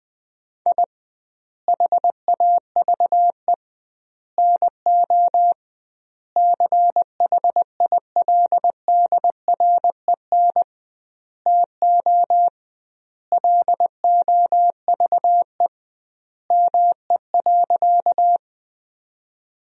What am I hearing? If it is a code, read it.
Morse code, heard as a keyed tone, about 21 wpm, 700 Hz. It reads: I HAVE NO CHILDREN TO LOVE ME.
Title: 2x Word-Spacing - Sentences from Top 400 Words - 20wpm